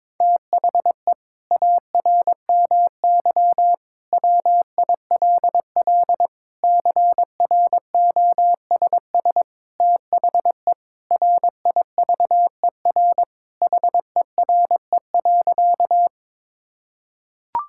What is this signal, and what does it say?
Morse code, heard as a keyed tone, about 22 wpm, 700 Hz. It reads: THE ARMY WILL CROSS THE RIVER HERE.
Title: Sentences from Top 1000 Words - Review Part 5 - 22wpm